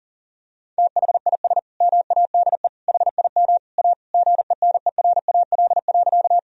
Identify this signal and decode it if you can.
Morse code, heard as a keyed tone, about 37 wpm, 700 Hz. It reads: THIS MADE HIM A GENERAL.
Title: Sentences from Top 1000 Words - Review Part 3 - 40wpm